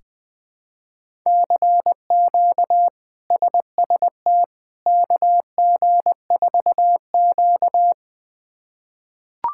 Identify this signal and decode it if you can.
Morse code, heard as a keyed tone, about 20 wpm, 700 Hz. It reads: CQ SST KG4Q
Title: K1USN SST QSO Practice - 20wpm